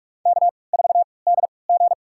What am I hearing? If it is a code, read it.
Morse code, heard as a keyed tone, about 46 wpm, spaced as for 25 wpm, 700 Hz. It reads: K4DG
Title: USA Call Signs - 4 Characters - 3x Character Spacing - 45wpm